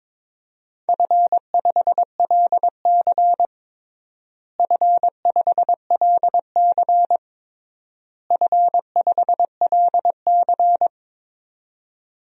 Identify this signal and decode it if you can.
Morse code, heard as a keyed tone, about 22 wpm, 700 Hz. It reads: F5LC F5LC F5LC
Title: DXCC-100 Call Signs - 4 Characters - Repeated 3x - 22wpm